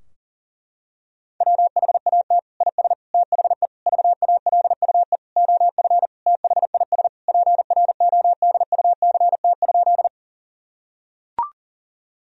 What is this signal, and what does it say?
Morse code, heard as a keyed tone, about 40 wpm, 700 Hz. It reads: WHAT IS THE VALUE OF THIS PRODUCT?
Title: Sentences from Top 1000 Words - Review Part 3 - 40wpm